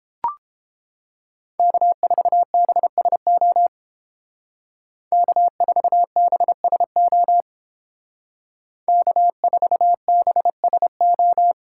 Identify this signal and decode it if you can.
Morse code, heard as a keyed tone, about 30 wpm, 700 Hz. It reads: K4BSO K4BSO K4BSO